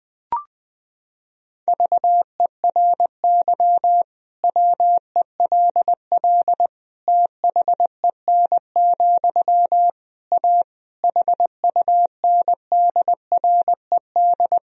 Morse code, heard as a keyed tone, about 20 wpm, 700 Hz. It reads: VERY WELL THEN, A HUNDRED
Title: Sentences from Top 1000 Words - Review Part 3 - 20wpm